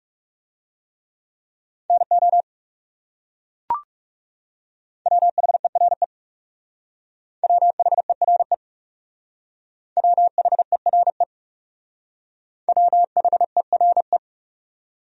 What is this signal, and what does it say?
Morse code, heard as a keyed tone, about 35 wpm, 700 Hz. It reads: NO WHERE WHERE WHERE WHERE
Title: Speed-Racing Top 200 Words 30wpm